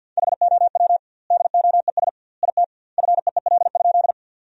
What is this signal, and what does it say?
Morse code, heard as a keyed tone, about 53 wpm, 700 Hz. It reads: HOW DOES IT FEEL?
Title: 2x Word-Spacing - Sentences from Top 400 Words - 50wpm